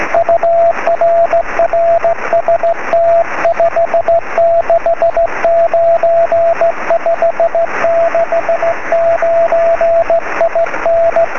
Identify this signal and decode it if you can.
Morse code, heard as a keyed tone, about 15 wpm, 670 Hz, with radio noise. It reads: UR RST 569 569 IN
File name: sm7lcb / 20 23cm / 1544 pa3bfk 23cm